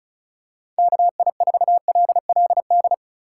Morse code, heard as a keyed tone, about 36 wpm, 700 Hz. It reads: KI4LLD
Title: USA Call Signs - 6 Characters - 3x Character Spacing - 35wpm